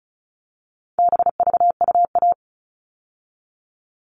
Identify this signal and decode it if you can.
Morse code, heard as a keyed tone, about 34 wpm, 700 Hz. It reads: BVUA
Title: All Cut Numbers - 4-Digit 35wpm